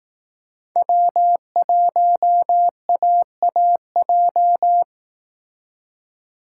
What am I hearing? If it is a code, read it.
Morse code, heard as a keyed tone, about 18 wpm, 700 Hz. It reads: W1AAJ
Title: Speed Racing - USA 1x3 Call Signs 15wpm